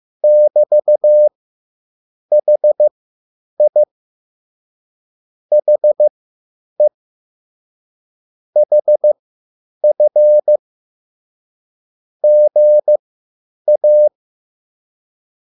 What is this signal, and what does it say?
Morse code, heard as a keyed tone, about 15 wpm, spaced as for 8 wpm, 600 Hz. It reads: <BT>HI HE HF GA